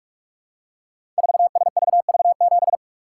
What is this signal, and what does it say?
Morse code, heard as a keyed tone, about 44 wpm, 700 Hz. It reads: 4SVV7
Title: Code Groups - 5 Letter:Numbers - 45wpm